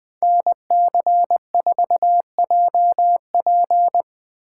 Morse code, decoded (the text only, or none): NC4JP